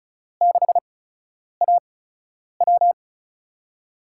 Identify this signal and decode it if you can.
Morse code, heard as a keyed tone, about 36 wpm, spaced as for 10 wpm, 700 Hz. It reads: 6AW
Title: USA Call Signs - 4 Characters - 8x Character Spacing - 35wpm